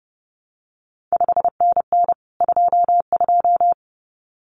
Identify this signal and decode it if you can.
Morse code, heard as a keyed tone, about 30 wpm, 700 Hz. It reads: 5NN 22
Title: CQ WW DX Contest Exchange Encore - 30wpm